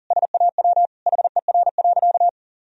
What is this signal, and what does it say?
Morse code, heard as a keyed tone, about 40 wpm, 700 Hz. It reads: SAW HER.
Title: Sentences from Top 1000 Words - Review Part 2 - 40wpm